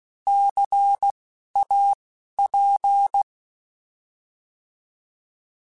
Morse code, heard as a keyed tone, about 16 wpm, spaced as for 12 wpm, 800 Hz. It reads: CAP
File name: Alphabetical / Alphabetical mp3s / Cs @10